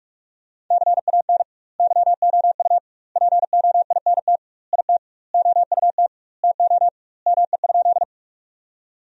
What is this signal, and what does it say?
Morse code, heard as a keyed tone, about 46 wpm, 700 Hz. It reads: CAN YOU POINT IT OUT TO ME?
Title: 2x Word-Spacing - Sentences from Top 200 Words - 45wpm